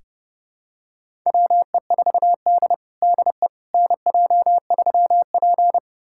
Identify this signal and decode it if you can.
Morse code, heard as a keyed tone, about 30 wpm, 700 Hz. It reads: WE4D DE NJ3P
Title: USA Callsign DE Callsign - 4 Characters - 30wpm